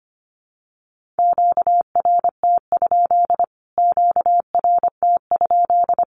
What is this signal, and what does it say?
Morse code, heard as a keyed tone, about 25 wpm, 700 Hz. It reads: QRT? QRT?
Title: Q-Codes Repeated 3x 25wpm